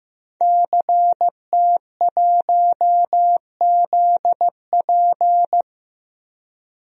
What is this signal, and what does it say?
Morse code, heard as a keyed tone, about 15 wpm, 700 Hz. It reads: CT1ZP
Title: DXCC-100 Call Signs - 5 Characters - 15wpm